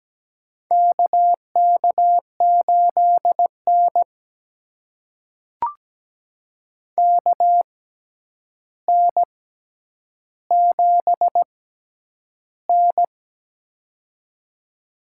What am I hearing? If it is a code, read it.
Morse code, heard as a keyed tone, about 17 wpm, spaced as for 9 wpm, 700 Hz. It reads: KK8N KN7N
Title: USA Call Signs - 4 Characters - 6x Character Spacing - 17wpm